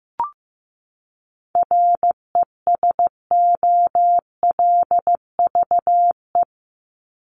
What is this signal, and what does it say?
Morse code, heard as a keyed tone, about 15 wpm, 700 Hz. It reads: RESOLVE
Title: Top 1751-2000 Words 15wpm